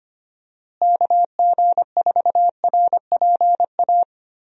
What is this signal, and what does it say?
Morse code, heard as a keyed tone, about 25 wpm, 700 Hz. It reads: KG4RPA
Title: USA Call Signs - 6 Characters - Repeated 3x - 25wpm